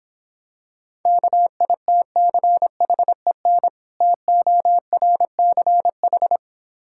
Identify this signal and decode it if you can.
Morse code, heard as a keyed tone, about 26 wpm, 700 Hz. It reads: KITCHEN TORCH